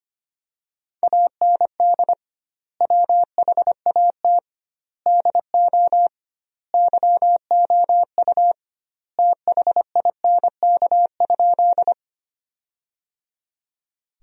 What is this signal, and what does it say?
Morse code, heard as a keyed tone, about 25 wpm, 700 Hz. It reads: AND WHAT DO YOU THINK?
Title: 2x Word-Spacing - Sentences from Top 200 Words - 25wpm